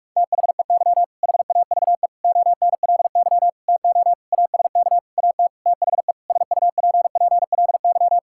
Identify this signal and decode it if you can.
Morse code, heard as a keyed tone, about 46 wpm, 700 Hz. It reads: THEY HAVE ONLY TO ASK AT THE SUPPLY
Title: Sentences from Top 1000 Words - Review Part 5 - 45wpm